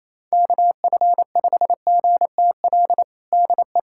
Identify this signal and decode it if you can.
Morse code, heard as a keyed tone, about 28 wpm, 700 Hz. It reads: KF5GTL DE
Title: USA Callsign DE Callsign 28wpm